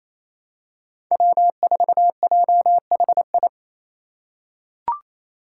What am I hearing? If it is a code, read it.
Morse code, heard as a keyed tone, about 28 wpm, 700 Hz. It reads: W4JHI
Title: Five Challenging Letter-Number Combinations 28wpm